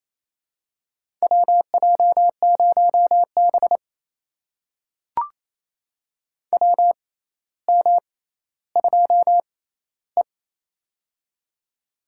Morse code, heard as a keyed tone, about 28 wpm, spaced as for 14 wpm, 700 Hz. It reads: WJ0B WM2E
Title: USA Call Signs - 4 Characters - 6x Character Spacing - 28wpm